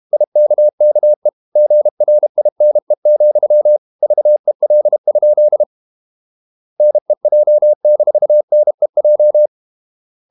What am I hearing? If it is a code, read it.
Morse code, heard as a keyed tone, about 32 wpm, 600 Hz. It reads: IKKE GRINE, VEL? NEJ-NEJ
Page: Ferie 32 wpm